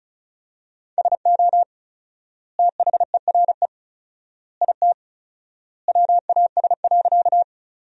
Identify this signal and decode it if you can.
Morse code, heard as a keyed tone, about 36 wpm, 700 Hz. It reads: SO THERE IT WAS.
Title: 4x Word-Spacing - Sentences from Top 100 Words - 35wpm